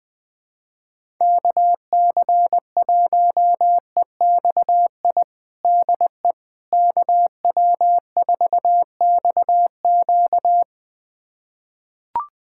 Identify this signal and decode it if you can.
Morse code, heard as a keyed tone, about 20 wpm, 700 Hz. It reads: KC1EXI DE KW4XQ
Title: USA Callsign DE Callsign 20wpm